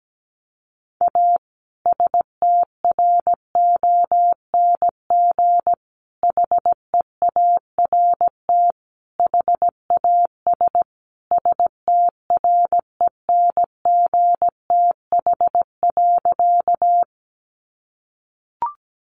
Morse code, heard as a keyed tone, about 17 wpm, 700 Hz. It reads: A STRONG HEART HAS STRENGTH.
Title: Sentences from Top 1300 Words 17wpm